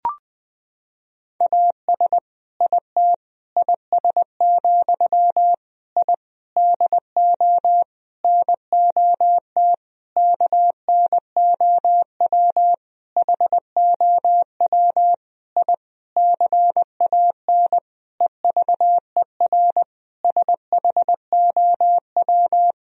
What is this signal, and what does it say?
Morse code, heard as a keyed tone, about 20 wpm, 700 Hz. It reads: AS IT IS, I DO NOT KNOW HOW I CAN EVER SHOW
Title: Sentences from Top 1000 Words - Review Part 2 - 20wpm